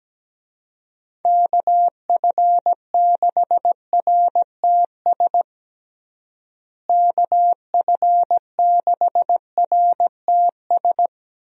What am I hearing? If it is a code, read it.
Morse code, heard as a keyed tone, about 17 wpm, 700 Hz. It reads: KF6RTS KF6RTS
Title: USA Call Signs - 6 Characters - Repeated 2x - 17wpm